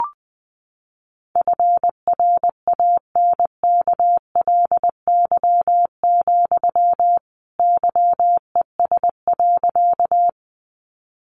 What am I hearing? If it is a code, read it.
Morse code, heard as a keyed tone, about 20 wpm, 700 Hz. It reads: FRANKLY, YES.
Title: A Sound Of Thunder 20wpm